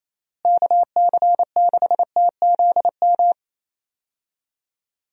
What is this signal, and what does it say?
Morse code, heard as a keyed tone, about 28 wpm, 700 Hz. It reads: KC6TZM
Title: USA Call Signs - 6 Characters - Repeated 2x - 28wpm